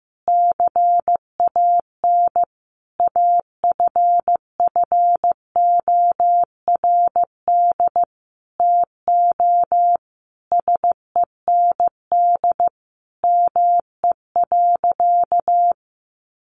Morse code, heard as a keyed tone, about 15 wpm, 700 Hz. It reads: CAN AFFORD TO SEND ME.
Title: Sentences from Top 1500 Words 15wpm